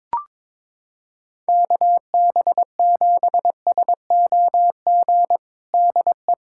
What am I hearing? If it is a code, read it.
Morse code, heard as a keyed tone, about 22 wpm, 700 Hz. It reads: KB7SOG DE
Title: USA Callsign DE Callsign 22wpm